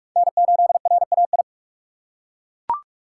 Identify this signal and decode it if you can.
Morse code, heard as a keyed tone, about 42 wpm, 700 Hz. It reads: N8RAI
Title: USA Call Signs - 5 Characters - 4x Character Spacing - 45wpm